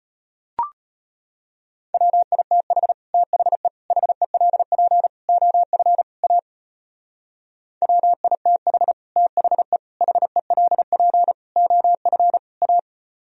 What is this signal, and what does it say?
Morse code, heard as a keyed tone, about 31 wpm, 700 Hz. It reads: WITH THE HELP OF A WITH THE HELP OF A
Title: Speed-Racing - Sets of 5 Words 25wpm